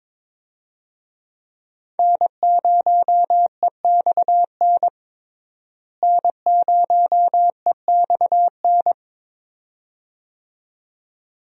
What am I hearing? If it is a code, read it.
Morse code, heard as a keyed tone, about 22 wpm, 700 Hz. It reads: N0EXN N0EXN
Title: USA Call Signs - 5 Characters - Repeated 2x - 22wpm